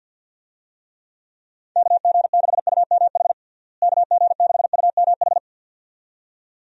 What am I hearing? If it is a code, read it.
Morse code, heard as a keyed tone, about 51 wpm, 700 Hz. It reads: KG6UMH KG6UMH
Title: USA Call Signs - 6 Characters - Repeated 2x - 50wpm